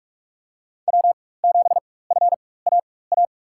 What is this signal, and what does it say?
Morse code, heard as a keyed tone, about 47 wpm, spaced as for 21 wpm, 700 Hz. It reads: W7FAA
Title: USA Call Signs - 4x Character Spacing - 45wpm